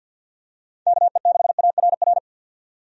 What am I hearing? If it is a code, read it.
Morse code, heard as a keyed tone, about 49 wpm, 700 Hz. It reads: KE6ARR
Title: USA Call Signs - 6 Characters - 50wpm